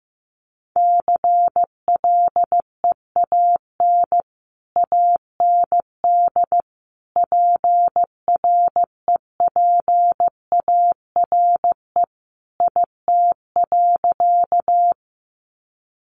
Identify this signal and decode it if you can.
Morse code, heard as a keyed tone, about 15 wpm, 700 Hz. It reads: CLEAN AND PREPARE IT.
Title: Sentences from Top 1100 Words 15wpm